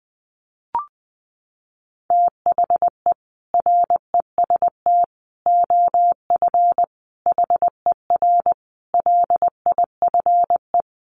Morse code, heard as a keyed tone, about 20 wpm, 700 Hz. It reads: THE REST OF HER LIFE
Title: Sets of 5 Words Encore 20wpm